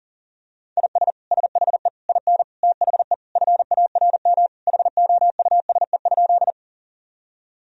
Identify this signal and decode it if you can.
Morse code, heard as a keyed tone, about 40 wpm, 700 Hz. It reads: IS SHE IN THE FARM HOUSE?
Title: Sentences from Top 1000 Words - Review Part 2 - 40wpm